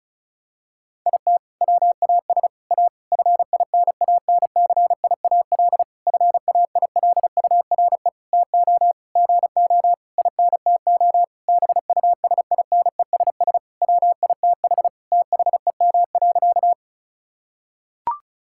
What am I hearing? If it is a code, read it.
Morse code, heard as a keyed tone, about 34 wpm, 700 Hz. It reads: IT WAS A FINANCIAL FAILURE TO GO INTO BUSINESS WITH THEM.
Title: Sentences from Top 1000 Words - Review Part 6 - 35wpm